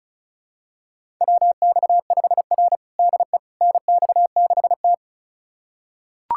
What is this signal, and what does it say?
Morse code, heard as a keyed tone, about 34 wpm, 700 Hz. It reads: WX5R DE NX6T
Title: USA Callsign DE Callsign - 4 Characters - 35wpm